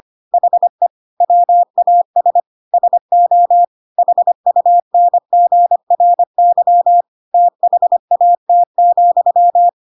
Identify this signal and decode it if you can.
Morse code, heard as a keyed tone, about 25 wpm, 700 Hz. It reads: HE WAS SO HUNGRY THAT,
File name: Pinocchio / Pinocchio ch6 25 25 ik1hge